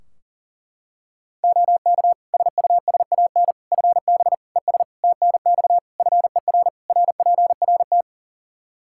Morse code, heard as a keyed tone, about 40 wpm, 700 Hz. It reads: OK SUSAN FB ES TNX FER RPRT
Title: Practice QSO - Exchange 2 - Info - 40wpm